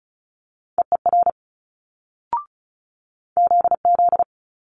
Morse code, heard as a keyed tone, about 36 wpm, 700 Hz. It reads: EER ZZ
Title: Multi-letter Phonemes Quick Fire 35wpm